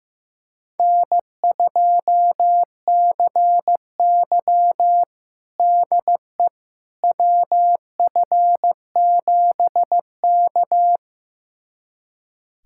Morse code, heard as a keyed tone, about 15 wpm, 700 Hz. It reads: N2CY DE WF7K
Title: USA Callsign DE Callsign - 4 Characters - 15wpm